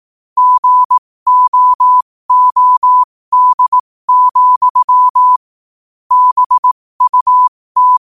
Morse code, heard as a keyed tone, about 18 wpm, spaced as for 15 wpm, 1000 Hz. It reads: GOOD, BUT
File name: qotdfiles / 15wpm